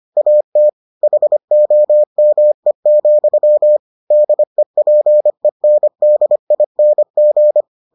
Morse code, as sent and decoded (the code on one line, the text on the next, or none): .- - / .... --- -- . --..-- / -.. .--. -. -.. .. -. --.
AT HOME, DEPENDING